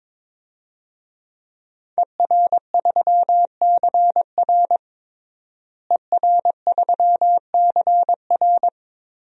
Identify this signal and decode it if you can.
Morse code, heard as a keyed tone, about 22 wpm, 700 Hz. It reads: ER3CR ER3CR
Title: DXCC-100 Call Signs - 5 Characters - Repeated 2x - 22wpm